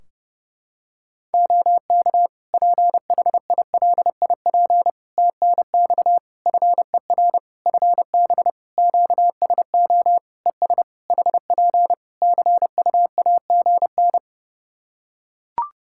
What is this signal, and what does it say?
Morse code, heard as a keyed tone, about 30 wpm, 700 Hz. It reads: OK PHILIP TNX FER FB QSO ES HP CUAGN
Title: Practice QSO - Exchange 4 - Ending - 30wpm